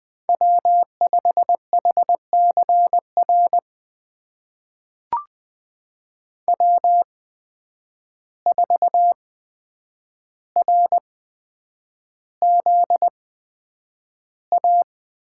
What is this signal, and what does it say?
Morse code, heard as a keyed tone, about 20 wpm, spaced as for 9 wpm, 700 Hz. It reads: W5HCR W4RZA
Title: USA Call Signs - 5 Characters - 8x Character Spacing - 20wpm